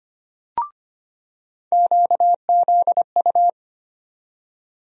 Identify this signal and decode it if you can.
Morse code, heard as a keyed tone, about 25 wpm, 700 Hz. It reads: QZU
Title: USA Call Sign Postfixes - 3 Letters 25wpm